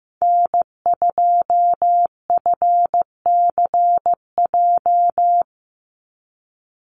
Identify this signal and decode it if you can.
Morse code, heard as a keyed tone, about 15 wpm, 700 Hz. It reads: N2FCJ
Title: USA Call Signs 15wpm